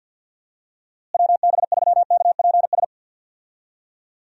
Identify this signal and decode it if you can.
Morse code, heard as a keyed tone, about 52 wpm, 700 Hz. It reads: WB3KPS
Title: USA Call Signs - 6 Characters - Repeated 2x - 50wpm